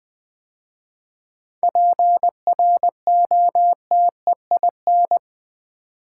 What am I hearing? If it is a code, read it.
Morse code, heard as a keyed tone, about 20 wpm, 700 Hz. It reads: PROTEIN
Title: 7 Letter Words - First 3 Letters Given - 20wpm